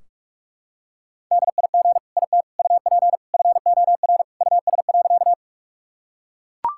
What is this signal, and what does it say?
Morse code, heard as a keyed tone, about 41 wpm, 700 Hz. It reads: DIG IT UP FOR US.
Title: Sentences from Top 1600 Words 45wpm